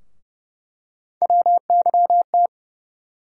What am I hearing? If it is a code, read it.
Morse code, heard as a keyed tone, about 31 wpm, 700 Hz. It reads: WYT
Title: USA Call Sign Postfixes - 3 Letters 30wpm